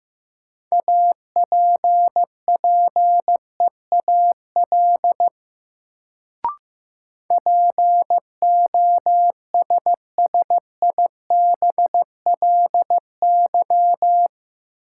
Morse code, heard as a keyed tone, about 15 wpm, 700 Hz. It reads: APPEAL POSSIBLY
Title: Sentences from Top 1300 Words - New Word Drill Alt - 15wpm